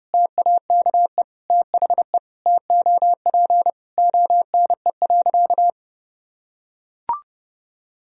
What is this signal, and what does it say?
Morse code, heard as a keyed tone, about 30 wpm, 700 Hz. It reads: TAKE THE TOP ONE.
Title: Sentences from Top 1000 Words - Review Part 2 - 30wpm